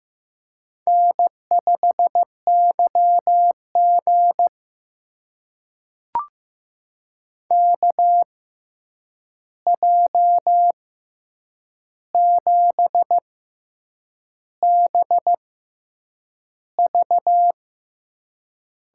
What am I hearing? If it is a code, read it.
Morse code, heard as a keyed tone, about 15 wpm, spaced as for 7 wpm, 700 Hz. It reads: N5YG KJ7BV
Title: USA Call Signs - 6x Character Spacing - 15wpm